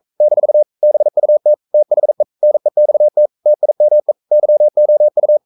Morse code, heard as a keyed tone, about 42 wpm, 600 Hz. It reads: - BUT THE NEXT TIME YOU
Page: Romanen Never af Ken Follet på engelsk.